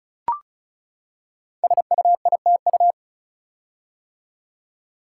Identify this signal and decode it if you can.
Morse code, heard as a keyed tone, about 36 wpm, 700 Hz. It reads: SUITU
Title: Code Groups - 5 Letters - 35wpm